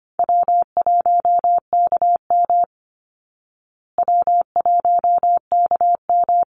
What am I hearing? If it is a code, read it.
Morse code, heard as a keyed tone, about 25 wpm, 700 Hz. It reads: W1KM W1KM
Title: Easy Call Signs Repeated 2x - 25wpm